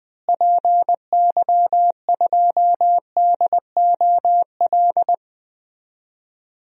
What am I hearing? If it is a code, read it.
Morse code, heard as a keyed tone, about 20 wpm, 700 Hz. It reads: PY2DOL
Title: DXCC-100 Call Signs - 6 Characters - Repeated 2x - 20wpm